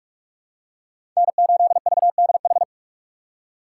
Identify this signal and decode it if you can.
Morse code, heard as a keyed tone, about 42 wpm, 700 Hz. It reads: N8VDH
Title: USA Call Signs - 5 Characters - 4x Character Spacing - 45wpm